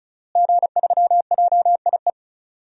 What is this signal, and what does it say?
Morse code, heard as a keyed tone, about 36 wpm, 700 Hz. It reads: G3JIE
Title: DXCC-100 Call Signs - 5 Characters - Repeated 3x - 35wpm